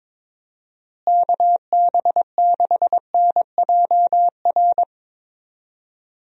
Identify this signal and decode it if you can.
Morse code, heard as a keyed tone, about 22 wpm, 700 Hz. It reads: KB6NJR
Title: USA Call Signs - 6 Characters - Repeated 3x - 22wpm